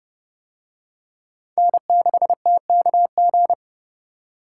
Morse code, heard as a keyed tone, about 30 wpm, 700 Hz. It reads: N6TKG